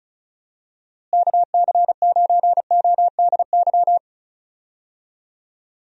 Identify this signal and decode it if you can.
Morse code, heard as a keyed tone, about 36 wpm, 700 Hz. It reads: KC9ODY
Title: USA Call Signs - 6 Characters - Repeated 3x - 35wpm